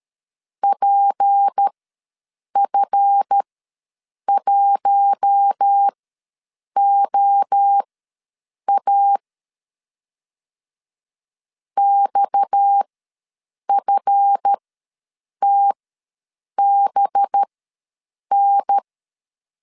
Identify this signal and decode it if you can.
Morse code, heard as a keyed tone, about 13 wpm, spaced as for 7 wpm, 800 Hz. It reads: PF1OA XFTBN
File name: LEZIONI / 7 GF9Z / 7 GF ws 12wpm 3140